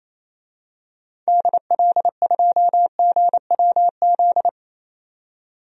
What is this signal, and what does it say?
Morse code, heard as a keyed tone, about 27 wpm, 700 Hz. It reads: DL2GWZ
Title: DXCC-100 Call Signs - 6 Characters - 28wpm